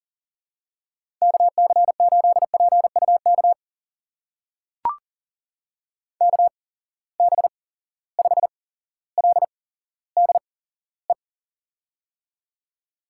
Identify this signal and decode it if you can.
Morse code, heard as a keyed tone, about 42 wpm, spaced as for 17 wpm, 700 Hz. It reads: KC8PUK KB5LDE
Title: USA Call Signs - 6 Characters - 8x Character Spacing - 40wpm